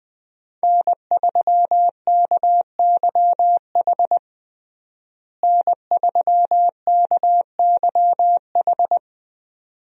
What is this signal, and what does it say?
Morse code, heard as a keyed tone, about 21 wpm, 700 Hz. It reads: N3KYH N3KYH
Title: USA Call Signs - 5 Characters - Repeated 2x - 20wpm